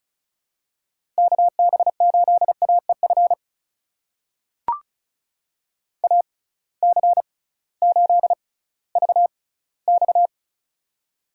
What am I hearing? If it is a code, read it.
Morse code, heard as a keyed tone, about 35 wpm, spaced as for 19 wpm, 700 Hz. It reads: KB8AEF AC8VX
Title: USA Call Signs - 6x Character Spacing - 35wpm